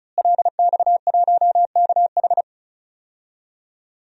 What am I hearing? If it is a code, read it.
Morse code, heard as a keyed tone, about 34 wpm, 700 Hz. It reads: LX1KH
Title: DXCC-100 Call Signs - 5 Characters - Repeated 3x - 35wpm